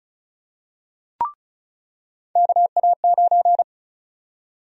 Morse code, heard as a keyed tone, about 36 wpm, 700 Hz. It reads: KA9
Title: USA Call Sign Prefixes - 35wpm